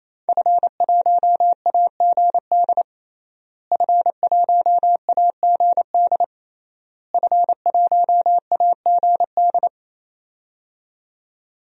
Morse code, heard as a keyed tone, about 28 wpm, 700 Hz. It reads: F1AGD F1AGD F1AGD
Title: DXCC-100 Call Signs - 5 Characters - Repeated 3x - 28wpm